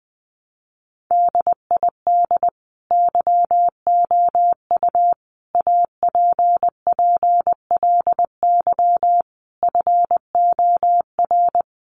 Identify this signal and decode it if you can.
Morse code, heard as a keyed tone, about 20 wpm, 700 Hz. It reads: DID YOU APPLY FOR
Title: Sentences from Top 1300 Words 20wpm